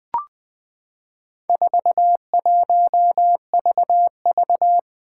Morse code, heard as a keyed tone, about 20 wpm, 700 Hz. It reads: 41VV
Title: Four Challenging Letter-Number Combinations 20wpm